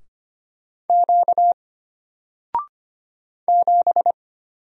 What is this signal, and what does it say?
Morse code, heard as a keyed tone, about 25 wpm, 700 Hz. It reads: Q 7
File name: Single-Letter-Number-25wpm